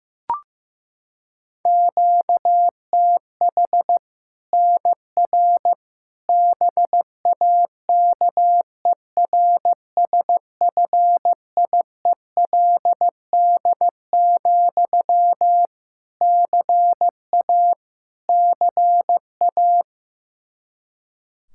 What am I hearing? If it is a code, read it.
Morse code, heard as a keyed tone, about 16 wpm, 700 Hz. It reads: QTH NR BAKERSFIELD, CA CA
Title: Practice QSO - Exchange 1 - Beginning - 15wpm